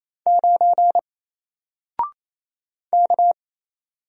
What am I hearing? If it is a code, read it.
Morse code, heard as a keyed tone, about 27 wpm, 700 Hz. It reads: 9 K
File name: Single-Letter-Number-28wpm